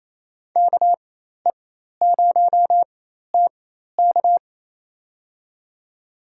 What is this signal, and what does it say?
Morse code, heard as a keyed tone, about 28 wpm, spaced as for 13 wpm, 700 Hz. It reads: KE0TK
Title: USA Call Signs - 5 Characters - 4x Character Spacing - 28wpm